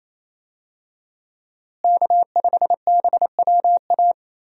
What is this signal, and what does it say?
Morse code, heard as a keyed tone, about 28 wpm, 700 Hz. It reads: K5BWA
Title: USA Call Signs - 5 Characters - Repeated 3x - 28wpm